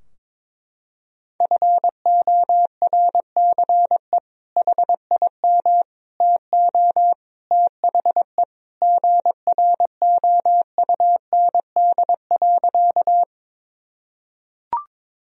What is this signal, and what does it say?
Morse code, heard as a keyed tone, about 22 wpm, 700 Hz. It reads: FORCE HIM TO THE GROUND.
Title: Sentences from Top 1000 Words - Review Part 3 - 22wpm